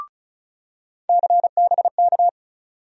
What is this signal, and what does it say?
Morse code, heard as a keyed tone, about 36 wpm, 700 Hz. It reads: CBK
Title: USA Call Sign Postfixes - 3 Letters 35wpm